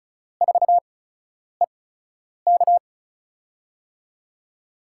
Morse code, heard as a keyed tone, about 36 wpm, spaced as for 10 wpm, 700 Hz. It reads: 4EK
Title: USA Call Signs - 5 Characters - 8x Character Spacing - 35wpm